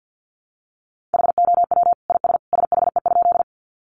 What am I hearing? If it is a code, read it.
Morse code, heard as a keyed tone, about 50 wpm, 700 Hz. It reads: HOW IS SHE?
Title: Sentences from Top 100 Words 50wpm